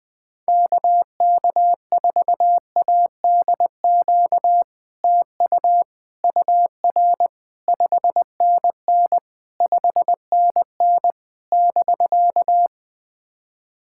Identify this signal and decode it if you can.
Morse code, heard as a keyed tone, about 18 wpm, 700 Hz. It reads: KK4ADQ TU UR 5NN 5NN <BK>
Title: POTA QSO Practice - 20wpm